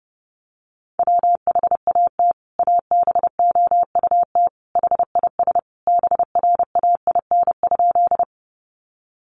Morse code, heard as a keyed tone, about 30 wpm, 700 Hz. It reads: WHAT ABOUT HIS BRAIN?
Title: Sentences from Top 1100 Words 30wpm